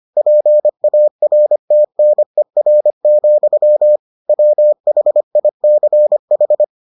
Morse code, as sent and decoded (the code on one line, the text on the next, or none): .--. .- .-. - -. .-. --..-- / .-- .... .. -.-. ....
PARTNER, WHICH